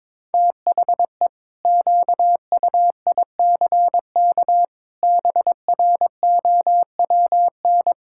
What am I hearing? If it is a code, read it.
Morse code, heard as a keyed tone, about 22 wpm, 700 Hz. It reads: THE QUICK BROWN
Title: Pangram Challenge v2 - 22wpm